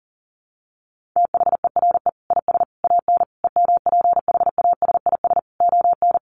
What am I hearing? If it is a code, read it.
Morse code, heard as a keyed tone, about 40 wpm, 700 Hz. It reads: THERE IS AN EMPHASIS ON
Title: Sentences from Top 1400 Words 40wpm